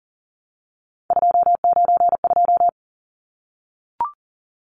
Morse code, heard as a keyed tone, about 40 wpm, 700 Hz. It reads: 292
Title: Numbers - 3-Digit 40wpm